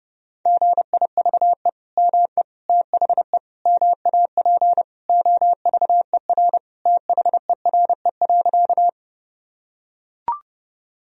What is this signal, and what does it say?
Morse code, heard as a keyed tone, about 32 wpm, 700 Hz. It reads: GIVE ME THE MAP OVER THERE.
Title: Sentences from Top 1000 Words - Review Part 2 - 30wpm